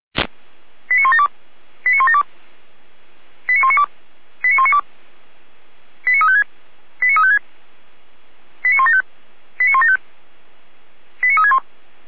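Five-tone selective call (ZVEI), heard as quick runs of five tones, tones of about 70 ms. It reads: 89172 89172 89192 89192 89356 89356 89175 89175 89361
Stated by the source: ZVEI five-tone call received by SDR.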